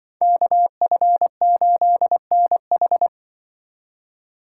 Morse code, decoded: KF8NH

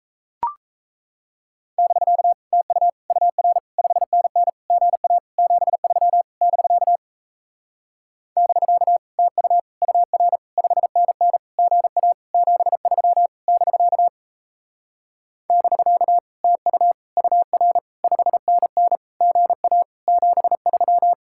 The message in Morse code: <BK> TU UR 5NN GA 73 <BK> <BK> TU UR 5NN GA 73 <BK> <BK> TU UR 5NN GA 73

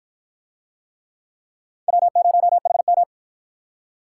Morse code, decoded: W0HM

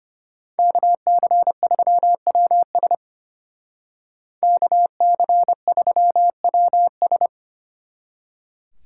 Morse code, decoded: KC3WS KC3WS

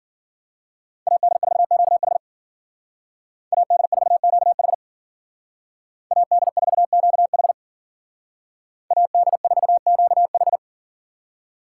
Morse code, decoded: AD4QH AD4QH AD4QH AD4QH

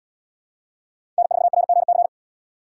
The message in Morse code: N4RGC

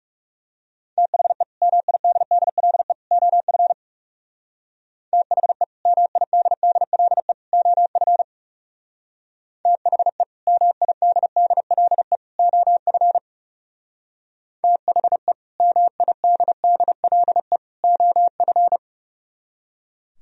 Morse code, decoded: THE MIDDLE OF THE MIDDLE OF THE MIDDLE OF THE MIDDLE OF

